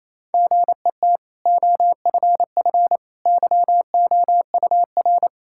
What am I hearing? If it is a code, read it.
Morse code, heard as a keyed tone, about 28 wpm, 700 Hz. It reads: GET OFF YOUR
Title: Sentences from Top 1600 Words 28wpm